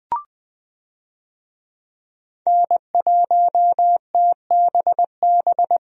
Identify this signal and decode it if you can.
Morse code, heard as a keyed tone, about 20 wpm, 700 Hz. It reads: N1TBB
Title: USA Call Signs - 5 Characters - 20wpm